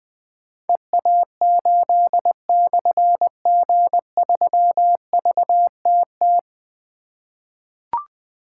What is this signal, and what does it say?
Morse code, heard as a keyed tone, about 20 wpm, 700 Hz. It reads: EA8/G3VTT
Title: Difficult Call Signs 20wpm